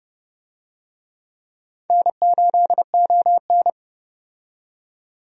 Morse code, heard as a keyed tone, about 30 wpm, 700 Hz. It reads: N8ON
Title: USA Call Signs - 4 Characters - 30wpm